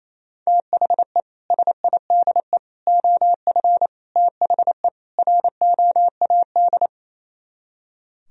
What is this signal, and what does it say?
Morse code, heard as a keyed tone, about 34 wpm, 700 Hz. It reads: THE SIDE OF THE ROAD
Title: Speed-Racing - Sets of 5 Words 28wpm